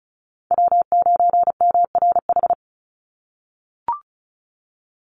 Morse code, heard as a keyed tone, about 34 wpm, 700 Hz. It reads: W9MRH
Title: NA SOTA Chaser Callsigns 35wpm